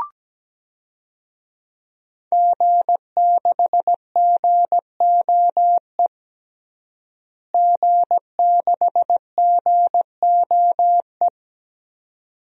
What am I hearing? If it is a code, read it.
Morse code, heard as a keyed tone, about 17 wpm, 700 Hz. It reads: G6GOE G6GOE
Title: DXCC-100 Call Signs - 5 Characters - Repeated 2x - 17wpm